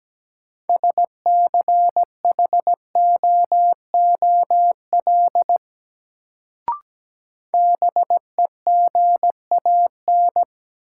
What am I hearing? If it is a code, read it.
Morse code, heard as a keyed tone, about 17 wpm, 700 Hz. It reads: SCHOOL BEGAN
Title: Sentences from Top 300 Words - New Word Drill - 17wpm